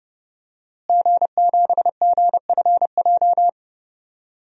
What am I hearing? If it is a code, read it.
Morse code, heard as a keyed tone, about 31 wpm, 700 Hz. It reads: G7GFJ
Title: DXCC-100 Call Signs - 5 Characters - Repeated 3x - 30wpm